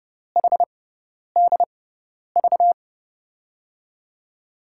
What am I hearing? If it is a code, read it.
Morse code, heard as a keyed tone, about 30 wpm, spaced as for 10 wpm, 690 Hz. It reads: HDV
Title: USA Call Signs - 5 Characters - 6x Character Spacing - 30wpm